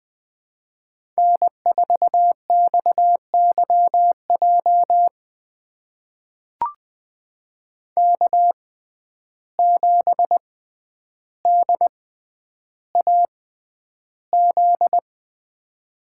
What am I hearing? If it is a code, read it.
Morse code, heard as a keyed tone, about 21 wpm, spaced as for 10 wpm, 700 Hz. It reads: N4XYJ K7DAZ
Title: USA Call Signs - 5 Characters - 6x Character Spacing - 20wpm